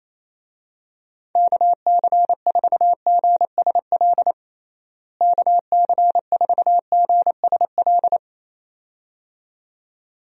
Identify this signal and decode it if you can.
Morse code, heard as a keyed tone, about 28 wpm, 700 Hz. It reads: KC4GSL KC4GSL
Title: USA Call Signs - 6 Characters - Repeated 2x - 28wpm